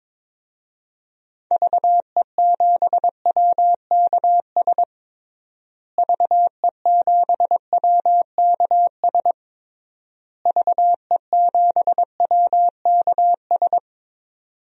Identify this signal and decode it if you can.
Morse code, heard as a keyed tone, about 21 wpm, 700 Hz. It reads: VE7WKS VE7WKS VE7WKS
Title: DXCC-100 Call Signs - 6 Characters - Repeated 3x - 22wpm